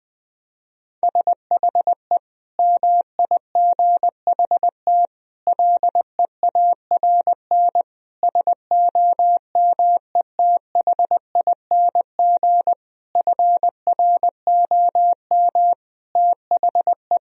Morse code, heard as a keyed tone, about 20 wpm, 700 Hz. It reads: SHE MIGHT LEARN SOMETHING FROM THE
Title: Sentences from Top 1000 Words - Review Part 4 - 20wpm